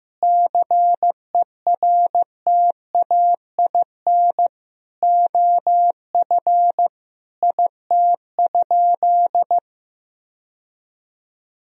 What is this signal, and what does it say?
Morse code, heard as a keyed tone, about 16 wpm, 700 Hz. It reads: CERTAIN OF IT?
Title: Sentences from Top 1000 Words - Review Part 3 - 15wpm